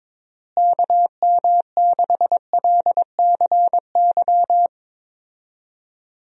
Morse code, heard as a keyed tone, about 22 wpm, 700 Hz. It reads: KM6LCY
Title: USA Call Signs - 6 Characters - Repeated 2x - 22wpm